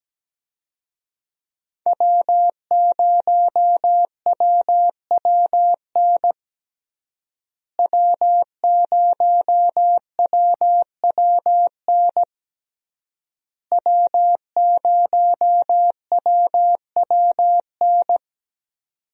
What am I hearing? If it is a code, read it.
Morse code, heard as a keyed tone, about 17 wpm, 700 Hz. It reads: W0WWN W0WWN W0WWN
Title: USA Call Signs - 5 Characters - Repeated 3x - 17wpm